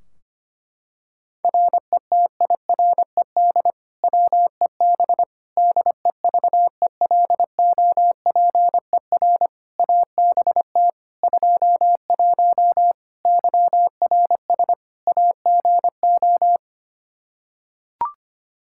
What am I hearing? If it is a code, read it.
Morse code, heard as a keyed tone, about 25 wpm, 700 Hz. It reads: RETIRED WEB DEVELOPER ABT 21 YRS AGO
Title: Practice QSO - Exchange 3 - More Info - 25wpm